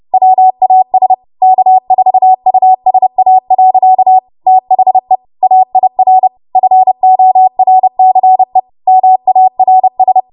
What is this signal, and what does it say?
Morse code, heard as a keyed tone, about 30 wpm, 750 Hz. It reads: WAS K4USA. THE AIR FORCE MARS
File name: Archive / 30 WPM / 30WPM